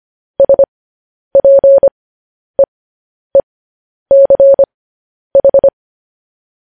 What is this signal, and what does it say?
Morse code, heard as a keyed tone, about 25 wpm, spaced as for 10 wpm, 550 Hz. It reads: SPEECH
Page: news headlines in morse code at 10 wpm